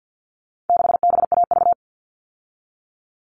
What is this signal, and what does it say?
Morse code, heard as a keyed tone, about 48 wpm, 700 Hz. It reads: 6BAV
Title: All Cut Numbers - 4-Digit 50wpm